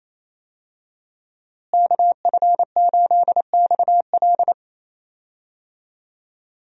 Morse code, heard as a keyed tone, about 28 wpm, 700 Hz. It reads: KF8XL
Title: USA Call Signs - 5 Characters - Repeated 2x - 28wpm